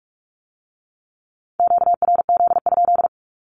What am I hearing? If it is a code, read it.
Morse code, heard as a keyed tone, about 46 wpm, 700 Hz. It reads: QRZ?
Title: Q-Codes with Spoken Meaning - 45wpm